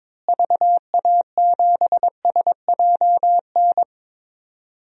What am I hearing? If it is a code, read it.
Morse code, heard as a keyed tone, about 22 wpm, 700 Hz. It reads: VA7SJN
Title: DXCC-100 Call Signs - 6 Characters - Repeated 3x - 22wpm